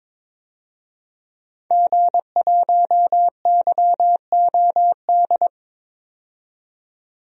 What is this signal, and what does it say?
Morse code, heard as a keyed tone, about 22 wpm, 700 Hz. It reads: G1YOD
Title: DXCC-100 Call Signs - 5 Characters - 22wpm